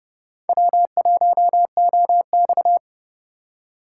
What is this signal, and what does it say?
Morse code, heard as a keyed tone, about 38 wpm, 700 Hz. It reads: W1OX
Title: Speed Racing - USA 1x2 Call Signs 30wpm